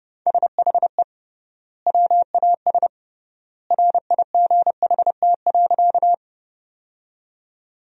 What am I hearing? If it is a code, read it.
Morse code, heard as a keyed tone, about 30 wpm, 700 Hz. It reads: SHE WAS RIGHT.
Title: 3x Word-Spacing - Sentences from Top 200 Words - 30wpm